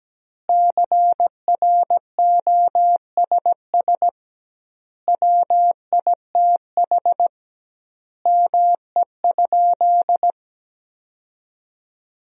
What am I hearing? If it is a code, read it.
Morse code, heard as a keyed tone, about 17 wpm, 700 Hz. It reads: CROSS WITH ME?
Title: 2x Word-Spacing - Sentences from Top 300 Words - 17wpm